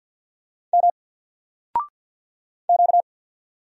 Morse code, heard as a keyed tone, about 50 wpm, 700 Hz. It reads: M Q
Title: Single Character (Letter:Number:Punctuation:Prosign) 50wpm